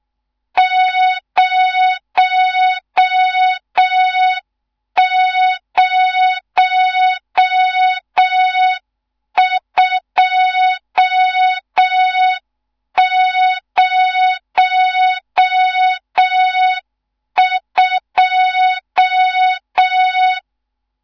Morse code, heard as a keyed tone, about 6 wpm, 3700 Hz. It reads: 00202
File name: Morze-HA2MN / hang / 14 280 HA2MN